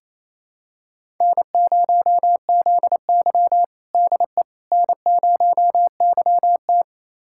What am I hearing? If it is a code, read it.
Morse code, heard as a keyed tone, about 28 wpm, 700 Hz. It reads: N0ZY DE N0YT
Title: USA Callsign DE Callsign - 4 Characters - 28wpm